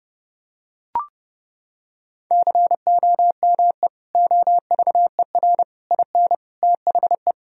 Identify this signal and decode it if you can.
Morse code, heard as a keyed tone, about 30 wpm, 700 Hz. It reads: COME OVER IN THE
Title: Sentences from Top 1000 Words - Review Part 1 - 30wpm